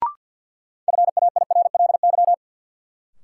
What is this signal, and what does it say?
Morse code, heard as a keyed tone, about 53 wpm, 700 Hz. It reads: FAIRLY
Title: Sentences from Top 1100 Words - New Word Drill - 50wpm